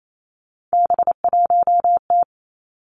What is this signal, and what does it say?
Morse code, heard as a keyed tone, about 28 wpm, 700 Hz. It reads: B1T
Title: Three Challenging Letter-Number Combinations 28wpm